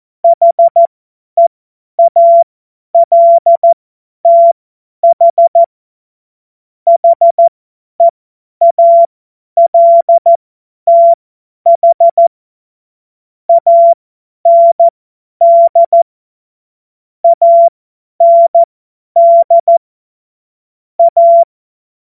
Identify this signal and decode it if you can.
Morse code, heard as a keyed tone, about 14 wpm, spaced as for 10 wpm, 680 Hz. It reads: HEALTH HEALTH AND AND A
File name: Morsekurs / Witze / Witz 14. Dezember 2021, 50 F70